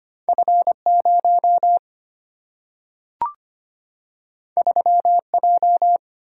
Morse code, heard as a keyed tone, about 25 wpm, 690 Hz. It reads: F0 3J
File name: Two Letter-Number Combinations - 25wpm